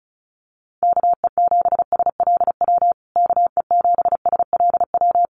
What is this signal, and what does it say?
Morse code, heard as a keyed tone, about 34 wpm, 700 Hz. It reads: KE7SLW KE7SLW
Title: USA Call Signs Repeated 2x 35wpm